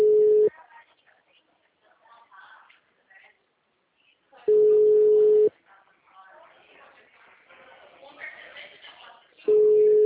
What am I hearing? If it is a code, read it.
Ringback tone (Europe, 425 Hz).
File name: out- -2011- - - .50322